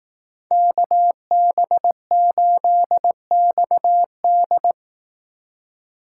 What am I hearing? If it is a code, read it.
Morse code, heard as a keyed tone, about 18 wpm, 700 Hz. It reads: KB8XD
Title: Speed Racing - USA Call Signs 15wpm